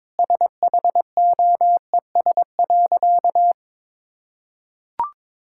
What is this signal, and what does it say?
Morse code, heard as a keyed tone, about 22 wpm, 700 Hz. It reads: SHOES.
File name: practice / 22wpm / Sentences with 3-Word Phrases - 22wpm